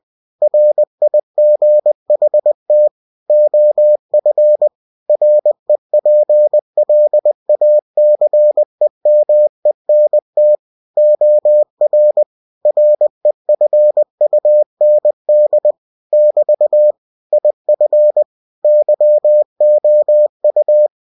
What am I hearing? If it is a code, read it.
Morse code, heard as a keyed tone, about 20 wpm, 600 Hz. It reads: RIGHT OF REPLACEMENT OR REFUND <BT> IF YOU
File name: cw-books / winnie-the-pooh / winnie-20wpm-page-0119